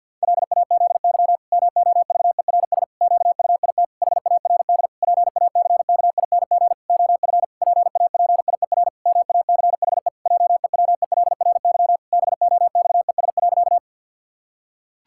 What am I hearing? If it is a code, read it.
Morse code, heard as a keyed tone, about 50 wpm, 700 Hz. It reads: LAZY MOVERS QUIT HARD PACKING OF PAPIER MACHE JEWELRY BOXES.
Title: Pangram Challenge v2 - 50wpm